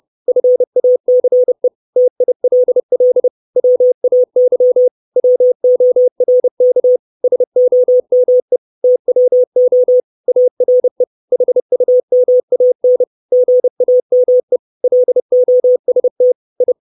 Morse code, heard as a keyed tone, about 30 wpm, 500 Hz. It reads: FACE TILL WAY WORK SOME TWO ARE HUMAN GAME LOST I
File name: 632 Words 30WPM